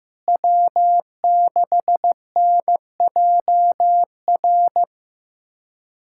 Morse code, heard as a keyed tone, about 15 wpm, 700 Hz. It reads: W6NJR
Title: USA Callsign DE Callsign - 5 Characters - 15wpm